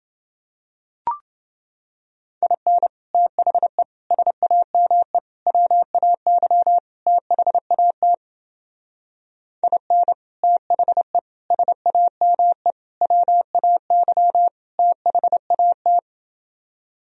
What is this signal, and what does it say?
Morse code, heard as a keyed tone, about 28 wpm, 700 Hz. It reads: IN THE SAME WAY THAT IN THE SAME WAY THAT